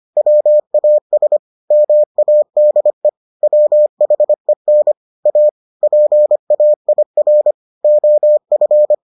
Morse code, heard as a keyed tone, about 25 wpm, 600 Hz. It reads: WAS MADE WHEN A PAIR OF